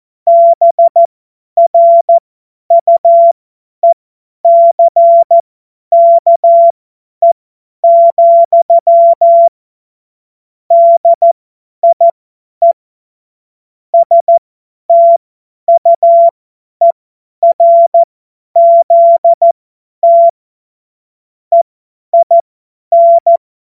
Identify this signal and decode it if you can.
Morse code, heard as a keyed tone, about 14 wpm, spaced as for 10 wpm, 680 Hz. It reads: BRUECKE, DIE STUERZT EIN
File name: Morsekurs / Witze / Witz 31. Mai 2022 50 F70